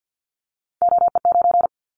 Morse code, heard as a keyed tone, about 50 wpm, 700 Hz. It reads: OE9
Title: Callsign Prefixes for Top 25 DXCC - 50wpm